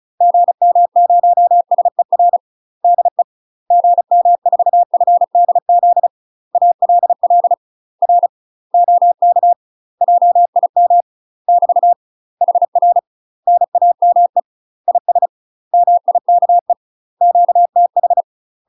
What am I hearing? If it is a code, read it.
Morse code, heard as a keyed tone, about 36 wpm, 700 Hz. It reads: GM0SER DE GM4FDZ ALL R OK JIM <BT> HR NAME IS MIKE QTH
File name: UK MASTER QSOs TRAINING / 35WPM / UK MASTER QSOs 0026